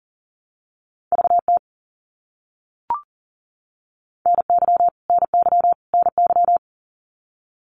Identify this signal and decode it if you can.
Morse code, heard as a keyed tone, about 40 wpm, 700 Hz. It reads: VT NY NY NY
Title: US States Repeated 3x 40wpm